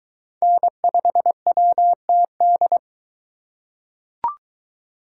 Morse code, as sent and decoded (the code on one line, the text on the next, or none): -. ..... .-- - -..
N5WTD